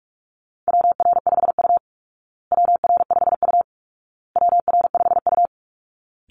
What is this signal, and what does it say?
Morse code, heard as a keyed tone, about 44 wpm, 700 Hz. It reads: WR5U WR5U WR5U
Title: Call Signs Repeated 2x - 45wpm